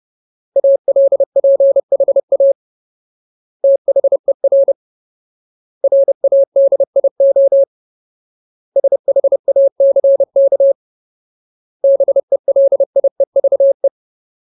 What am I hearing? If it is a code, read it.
Morse code, heard as a keyed tone, about 30 wpm, 550 Hz. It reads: ALPHA THER RADIO SHACK BELIEVE
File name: Ultimate Word List / 30WPM Ultimate